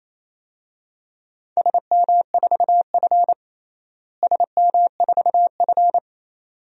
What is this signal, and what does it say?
Morse code, heard as a keyed tone, about 28 wpm, 700 Hz. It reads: SM4F SM4F
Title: DXCC-100 Call Signs - 4 Characters - Repeated 2x - 28wpm